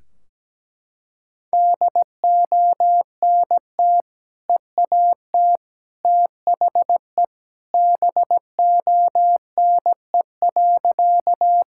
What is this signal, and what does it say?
Morse code, heard as a keyed tone, about 17 wpm, 700 Hz. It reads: DONT EAT THE BONE.
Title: Sentences from Top 1600 Words 17wpm